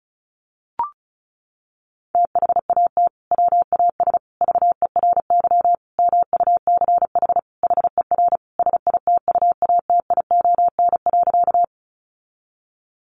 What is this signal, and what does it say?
Morse code, heard as a keyed tone, about 34 wpm, 700 Hz. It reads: THAT WAS VERY MUCH HER SITUATION.
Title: Sentences from Top 1200 Words 35wpm